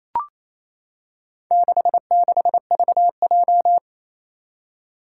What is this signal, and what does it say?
Morse code, heard as a keyed tone, about 27 wpm, 700 Hz. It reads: 66VJ
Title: Four Challenging Letter-Number Combinations 28wpm